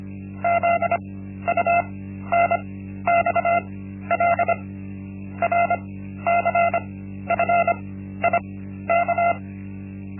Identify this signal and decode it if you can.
Morse code, heard as a keyed tone, about 25 wpm, spaced as for 13 wpm, 690 Hz. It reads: ZUNXL RCFIK
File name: Linksmakalnis-1